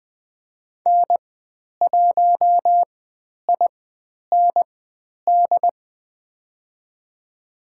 Morse code, heard as a keyed tone, about 20 wpm, spaced as for 10 wpm, 700 Hz. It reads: N1IND
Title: 095 Lesson 16g - Letter c - Call Signs 20wpm